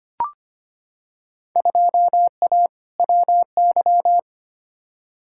Morse code, heard as a keyed tone, about 25 wpm, 700 Hz. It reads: 2A WY
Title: Field Day Exchanges - 25wpm